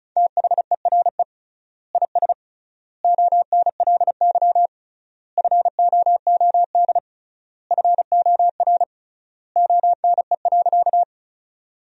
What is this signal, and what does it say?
Morse code, heard as a keyed tone, about 34 wpm, 700 Hz. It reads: THERE IS ONLY FOOD FOR ONE.
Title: 3x Word-Spacing - Sentences from Top 300 Words - 35wpm